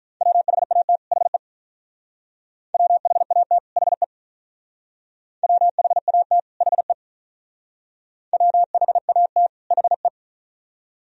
Morse code, heard as a keyed tone, about 45 wpm, 700 Hz. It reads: WHAT HE WHAT HE WHAT HE WHAT HE